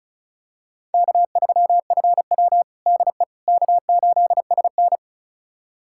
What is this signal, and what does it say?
Morse code, heard as a keyed tone, about 36 wpm, 700 Hz. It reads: K3FW DE K8SN
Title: USA Callsign DE Callsign - 4 Characters - 35wpm